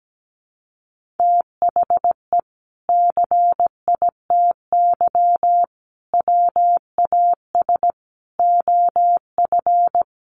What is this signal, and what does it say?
Morse code, heard as a keyed tone, about 17 wpm, 700 Hz. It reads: THE CITY WAS OF
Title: Sentences from Top 1500 Words 17wpm